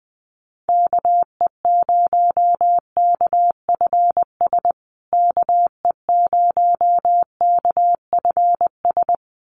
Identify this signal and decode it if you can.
Morse code, heard as a keyed tone, about 20 wpm, 700 Hz. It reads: KE0KFS KE0KFS
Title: USA Call Signs Repeated 2x 20wpm